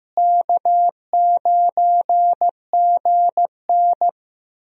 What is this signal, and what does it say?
Morse code, heard as a keyed tone, about 15 wpm, 700 Hz. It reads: K9GN
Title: USA Call Signs - 4 Characters - Repeated 3x - 15wpm